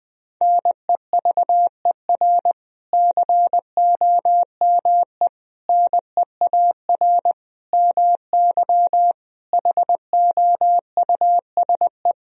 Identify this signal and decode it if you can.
Morse code, heard as a keyed tone, about 20 wpm, 700 Hz. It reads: NEVER COME NEAR MY HOUSE
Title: Sentences from Top 1000 Words - Review Part 1 - 20wpm